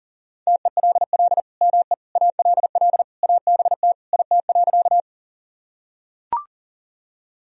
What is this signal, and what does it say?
Morse code, heard as a keyed tone, about 37 wpm, 700 Hz. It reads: TELL ME ALL ABT IT.
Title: Sentences from Top 200 Words and Abbreviations 40wpm